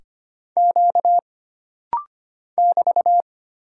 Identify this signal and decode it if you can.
Morse code, heard as a keyed tone, about 25 wpm, 700 Hz. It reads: Q <BT>
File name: Single Character (Letter:Number:Punctuation:Prosign) - Rapid-Fire - 25wpm